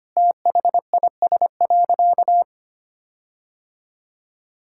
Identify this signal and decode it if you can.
Morse code, heard as a keyed tone, about 25 wpm, 700 Hz. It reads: THIS.
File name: practice / 25wpm / Sentences with 4-Word Phrases - 25wpm